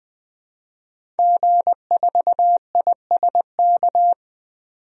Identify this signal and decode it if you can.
Morse code, heard as a keyed tone, about 20 wpm, 700 Hz. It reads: G4ISK
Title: DXCC-100 Call Signs - 5 Characters - 20wpm